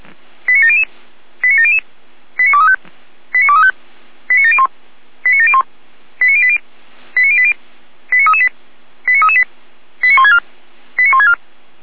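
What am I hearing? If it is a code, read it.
Five-tone selective call (ZVEI), heard as quick runs of five tones, tones of about 80 ms. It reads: 89700 89700 89235 89235 89701 89701 89080 89080 89338 89338 89164 89164
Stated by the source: Five-tone ZVEI selective call received on an SDR.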